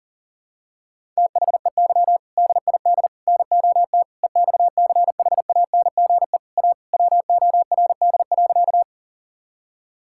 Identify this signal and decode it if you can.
Morse code, heard as a keyed tone, about 40 wpm, 700 Hz. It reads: THEY DID NOT EXCHANGE A WORD.
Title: Sentences from Top 1000 Words - Review Part 6 - 40wpm